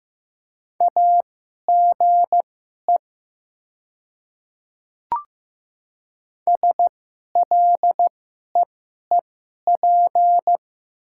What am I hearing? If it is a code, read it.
Morse code, heard as a keyed tone, about 15 wpm, spaced as for 11 wpm, 700 Hz. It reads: AGE SLEEP
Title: 2x Character Spacing - Top 500 Words - 15wpm